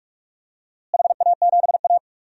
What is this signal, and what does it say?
Morse code, heard as a keyed tone, about 46 wpm, 700 Hz. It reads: HA7A
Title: DXCC-100 Call Signs - 4 Characters - Repeated 2x - 45wpm